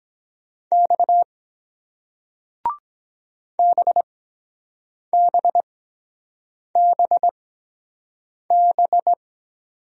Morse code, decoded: X BBBB

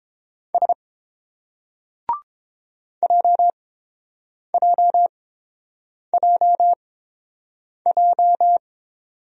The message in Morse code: S JJJJ

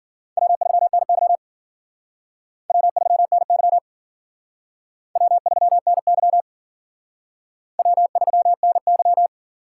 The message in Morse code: W3NY W3NY W3NY W3NY

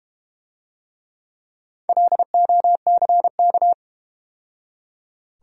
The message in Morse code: LOCK

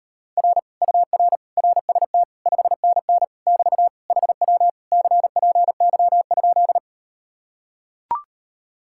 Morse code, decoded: R UR RST 5NN <BT> HW CPY?

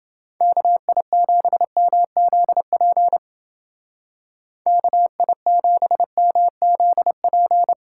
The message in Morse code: KI7MZP KI7MZP

